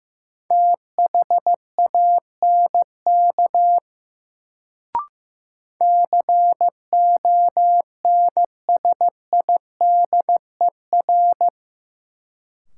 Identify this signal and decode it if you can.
Morse code, heard as a keyed tone, about 16 wpm, 700 Hz. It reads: THANK CONSIDER
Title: Sentences from Top 600 Words - New Word Drill - 15wpm